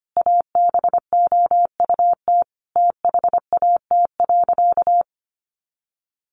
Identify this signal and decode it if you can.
Morse code, heard as a keyed tone, about 25 wpm, 700 Hz. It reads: ABOUT THAT.
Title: Sentences from Top 1000 Words 25wpm